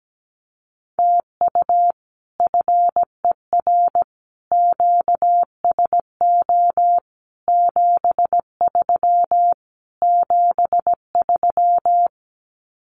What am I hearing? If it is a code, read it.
Morse code, heard as a keyed tone, about 17 wpm, 700 Hz. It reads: TU FER QSO 73 73
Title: Common QSO Phrases Encore 17wpm